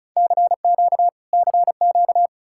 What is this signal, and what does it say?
Morse code, heard as a keyed tone, about 46 wpm, 700 Hz. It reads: CQ CQ